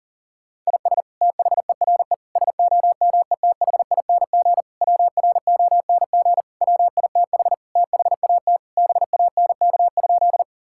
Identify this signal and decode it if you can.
Morse code, heard as a keyed tone, about 40 wpm, 700 Hz. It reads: IS THERE SOMETHING WRONG WITH THAT BANK?
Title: Sentences from Top 1000 Words - Review Part 6 - 40wpm